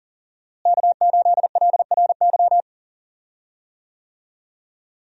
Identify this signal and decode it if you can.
Morse code, heard as a keyed tone, about 40 wpm, 700 Hz. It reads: K8LRY
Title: USA Call Signs - 5 Characters - Repeated 3x - 40wpm